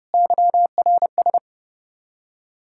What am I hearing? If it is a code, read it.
Morse code, heard as a keyed tone, about 30 wpm, 700 Hz. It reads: YRS
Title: Practice QSO - Exchange 3 - More Info - 30wpm